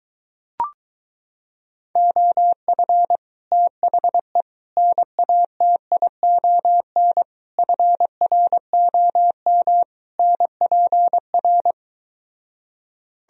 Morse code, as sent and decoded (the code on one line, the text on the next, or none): --- ..-. / - .... / -. .- - .. --- -. / ..-. .-. --- -- / -. .--. .-.
OF THE NATION FROM NPR